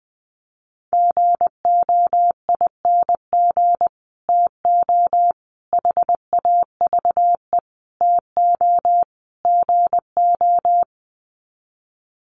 Morse code, heard as a keyed tone, about 20 wpm, 700 Hz. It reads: GOING TO HAVE TO GO
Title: Sets of 5 Words Encore 20wpm